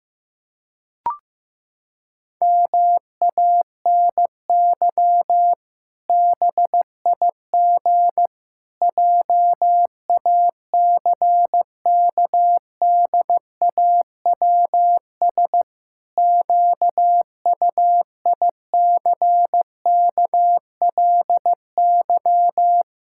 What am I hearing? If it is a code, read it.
Morse code, heard as a keyed tone, about 14 wpm, 700 Hz. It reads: MANY BIG JACKDAWS QUICKLY
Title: Pangram Challenge v2 - 15wpm